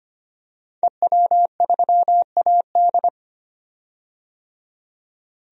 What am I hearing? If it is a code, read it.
Morse code, heard as a keyed tone, about 25 wpm, 700 Hz. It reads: EW3AD
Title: DXCC-100 Call Signs - 5 Characters - 25wpm